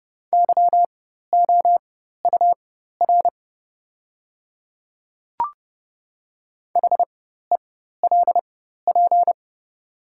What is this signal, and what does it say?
Morse code, heard as a keyed tone, about 30 wpm, spaced as for 14 wpm, 700 Hz. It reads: YOUR HELP
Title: 4x Character Spacing - Top 500 Words - 30wpm